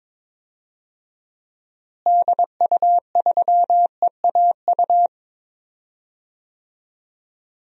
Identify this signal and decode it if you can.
Morse code, heard as a keyed tone, about 22 wpm, 700 Hz. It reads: DU3EAU
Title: DXCC-100 Call Signs - 6 Characters - Repeated 2x - 22wpm